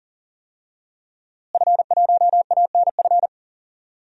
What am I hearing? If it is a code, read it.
Morse code, heard as a keyed tone, about 40 wpm, 700 Hz. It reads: F1ANF
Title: DXCC-100 Call Signs - 5 Characters - Repeated 3x - 40wpm